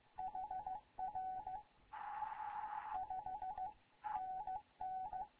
Morse code, heard as a keyed tone, about 15 wpm, spaced as for 13 wpm, 750 Hz, with radio noise. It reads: HR 5NN
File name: Morze-HA2MN / hangp / 2 11 GYAKORLAT